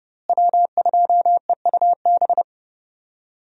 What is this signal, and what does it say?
Morse code, heard as a keyed tone, about 29 wpm, 700 Hz. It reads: W2EUB
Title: USA Call Signs - 5 Characters - Repeated 3x - 30wpm